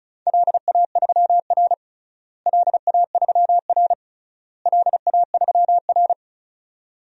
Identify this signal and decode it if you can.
Morse code, heard as a keyed tone, about 36 wpm, 700 Hz. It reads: LA3R LA3R LA3R
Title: DXCC-100 Call Signs - 4 Characters - Repeated 3x - 35wpm